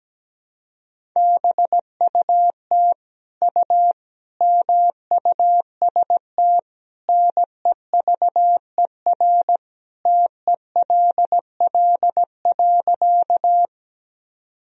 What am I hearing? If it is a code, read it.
Morse code, heard as a keyed tone, about 17 wpm, 700 Hz. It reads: BUT U MUST NEVER TELL.
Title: Sentences from Top 300 Words and Abbreviations 17wpm